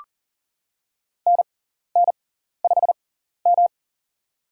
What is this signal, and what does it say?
Morse code, heard as a keyed tone, about 40 wpm, spaced as for 14 wpm, 700 Hz. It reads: NN5M
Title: USA Call Signs - 6x Character Spacing - 40wpm